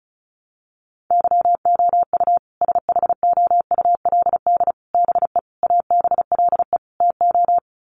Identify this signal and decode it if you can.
Morse code, heard as a keyed tone, about 34 wpm, 700 Hz. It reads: YOU SHOULD BE ABLE TO
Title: Sets of 5 Words Encore 35wpm